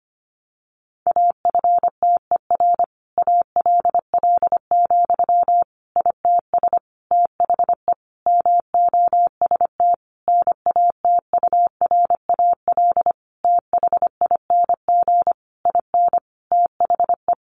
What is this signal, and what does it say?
Morse code, heard as a keyed tone, about 25 wpm, 700 Hz. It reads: AFTER ALL, ITS THE MOST NATURAL THING IN THE
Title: Sentences from Top 700 Words 25wpm